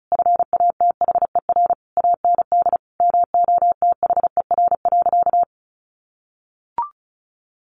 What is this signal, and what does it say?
Morse code, heard as a keyed tone, about 34 wpm, 700 Hz. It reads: FATHER AND MOTHER.
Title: Sentences from Top 1100 Words 35wpm